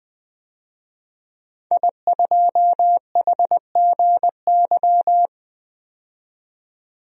Morse code, heard as a keyed tone, about 21 wpm, 700 Hz. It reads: I2HGY
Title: DXCC-100 Call Signs - 5 Characters - 20wpm